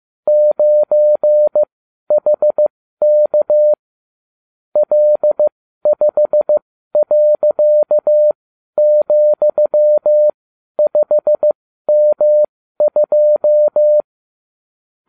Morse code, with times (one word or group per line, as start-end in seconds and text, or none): 0.3-3.7 9HK
4.8-14.0 L5.,5M2